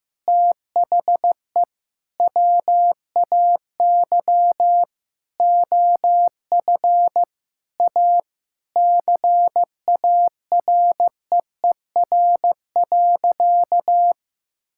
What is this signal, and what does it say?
Morse code, heard as a keyed tone, about 16 wpm, 700 Hz. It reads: THE WAY OF A CAREER.
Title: Sentences from Top 1000 Words - Review Part 6 - 15wpm